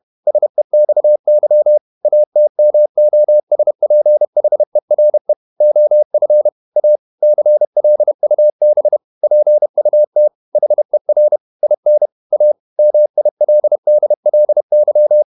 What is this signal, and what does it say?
Morse code, heard as a keyed tone, about 31 wpm, 600 Hz. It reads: SEXY ATMOSPHERE OF A CLUB PUT HER IN A MILDLY
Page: Never 31wpm | CW med Gnister